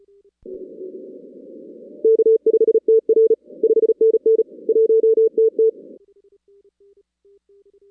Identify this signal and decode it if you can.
Morse code, heard as a keyed tone, about 34 wpm, 450 Hz. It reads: K5TR 5NN 1TT
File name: 2016 arrl dx cw / 2016-ARRL-DX-CW-K5TR- - 0-17